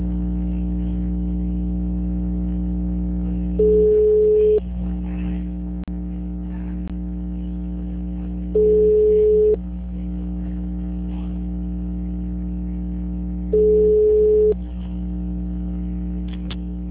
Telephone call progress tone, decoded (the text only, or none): ringback tone